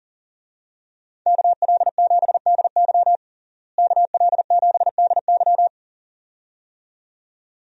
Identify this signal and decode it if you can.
Morse code, heard as a keyed tone, about 40 wpm, 700 Hz. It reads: KL7DY KL7DY
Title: DXCC-100 Call Signs - 5 Characters - Repeated 2x - 40wpm